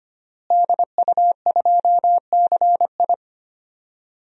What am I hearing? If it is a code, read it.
Morse code, heard as a keyed tone, about 25 wpm, 700 Hz. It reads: DU2CI
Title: DXCC-100 Call Signs - 5 Characters - Repeated 3x - 25wpm